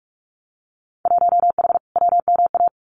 Morse code, heard as a keyed tone, about 46 wpm, 700 Hz. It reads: 1H WMA
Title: Winter Field Day Exchanges 45wpm